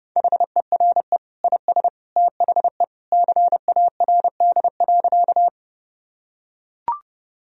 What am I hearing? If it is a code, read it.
Morse code, heard as a keyed tone, about 29 wpm, 700 Hz. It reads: HERE IS THE CARD.
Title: Sentences from Top 1000 Words - Review Part 4 - 30wpm